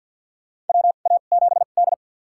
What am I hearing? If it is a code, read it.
Morse code, heard as a keyed tone, about 51 wpm, spaced as for 36 wpm, 700 Hz. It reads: WA7D
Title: USA Call Signs - 4 Characters - 2x Character Spacing - 50wpm